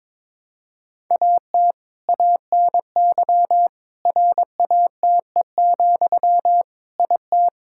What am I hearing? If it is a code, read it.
Morse code, heard as a keyed tone, about 22 wpm, 700 Hz. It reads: AT ANY RATE, IT
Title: Sentences from Top 1600 Words 22wpm